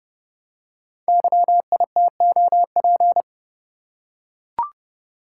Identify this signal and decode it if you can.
Morse code, heard as a keyed tone, about 31 wpm, 700 Hz. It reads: YITOP
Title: Code Groups - 5 Letters - 30wpm